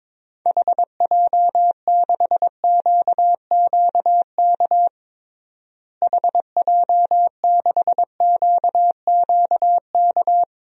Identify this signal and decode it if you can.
Morse code, heard as a keyed tone, about 21 wpm, 700 Hz. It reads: HJ6QQK HJ6QQK
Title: DXCC-100 Call Signs - 6 Characters - Repeated 3x - 22wpm